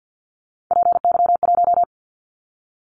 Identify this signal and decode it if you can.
Morse code, heard as a keyed tone, about 50 wpm, 700 Hz. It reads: PY1
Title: Callsign Prefixes for Top 100 DXCC - 50wpm